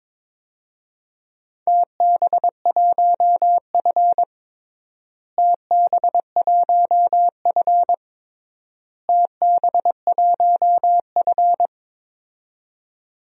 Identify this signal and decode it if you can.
Morse code, heard as a keyed tone, about 22 wpm, 700 Hz. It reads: TB1F TB1F TB1F
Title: DXCC-100 Call Signs - 4 Characters - Repeated 3x - 22wpm